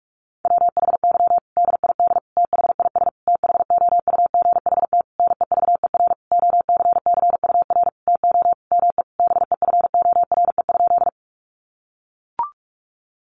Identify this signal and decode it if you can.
Morse code, heard as a keyed tone, about 44 wpm, 700 Hz. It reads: WHY DID THIS THOUGHT NEVER OCCUR TO ME BEFORE?
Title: Sentences from Top 900 Words 45wpm